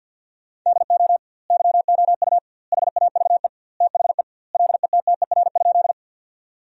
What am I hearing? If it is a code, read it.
Morse code, heard as a keyed tone, about 50 wpm, 700 Hz. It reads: DO YOU HAVE THE LETTER?
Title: 2x Word-Spacing - Sentences from Top 300 Words - 50wpm